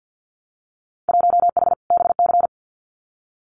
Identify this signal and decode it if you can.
Morse code, heard as a keyed tone, about 50 wpm, 700 Hz. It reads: 1H BC
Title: Winter Field Day Exchanges Encore 50wpm